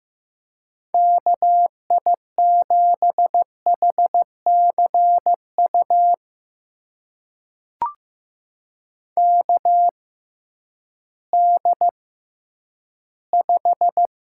Morse code, heard as a keyed tone, about 15 wpm, spaced as for 10 wpm, 700 Hz. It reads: KI7HCU KD5
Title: USA Call Signs - 6 Characters - 6x Character Spacing - 15wpm